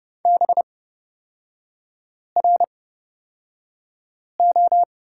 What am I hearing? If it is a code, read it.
Morse code, heard as a keyed tone, about 30 wpm, spaced as for 5 wpm, 700 Hz. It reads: BRO
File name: Lesson 36e - BT - Three Characters 30wpm